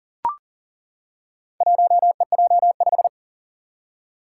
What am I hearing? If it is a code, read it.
Morse code, heard as a keyed tone, about 37 wpm, 700 Hz. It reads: 1EJ5
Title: Four Challenging Letter-Number Combinations 40wpm